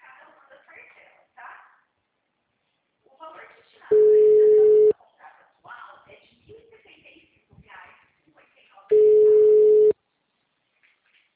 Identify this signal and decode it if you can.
Ringback tone (Europe, 425 Hz), with speech.
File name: out- -2007- - - .31516